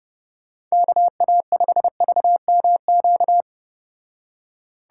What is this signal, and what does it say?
Morse code, heard as a keyed tone, about 34 wpm, 700 Hz. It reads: KA5VMQ